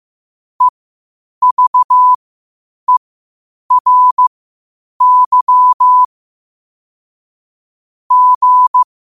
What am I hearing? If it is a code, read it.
Morse code, heard as a keyed tone, about 15 wpm, spaced as for 8 wpm, 1000 Hz. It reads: EVERY G
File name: qotdfiles / 8wpm